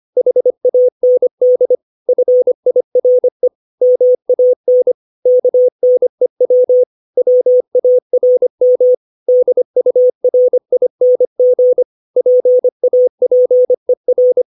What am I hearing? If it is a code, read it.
Morse code, heard as a keyed tone, about 25 wpm, 500 Hz. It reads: HAND FIRE MAN KNEW WARM DURING PAPER
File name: 632 words / 632 Words 25WPM